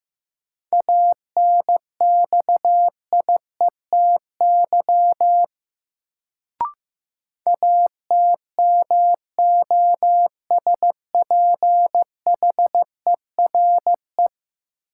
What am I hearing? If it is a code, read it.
Morse code, heard as a keyed tone, about 15 wpm, 700 Hz. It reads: ANXIETY ATMOSPHERE
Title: Sentences from Top 1600 Words - New Word Drill Alt - 15wpm